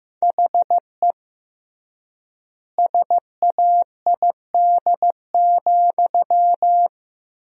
Morse code, heard as a keyed tone, about 16 wpm, 700 Hz. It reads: HE SAID,
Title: 3x Word-Spacing - Sentences from Top 400 Words - 15wpm